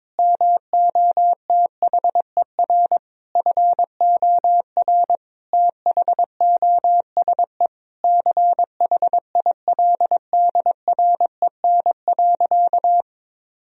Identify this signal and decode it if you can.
Morse code, heard as a keyed tone, about 21 wpm, 700 Hz. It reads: MOTHER FOR THOSE CHILDREN.
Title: Sentences from Top 1000 Words - Review Part 1 - 22wpm